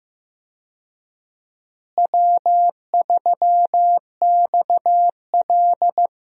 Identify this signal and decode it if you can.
Morse code, heard as a keyed tone, about 15 wpm, 700 Hz. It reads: W3XL
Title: USA Call Signs - 4 Characters - Repeated 3x - 15wpm